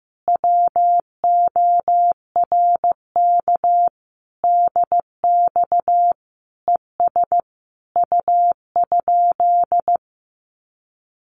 Morse code, heard as a keyed tone, about 15 wpm, 700 Hz. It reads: WORK DX ES U?
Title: Common QSO Phrases Encore 15wpm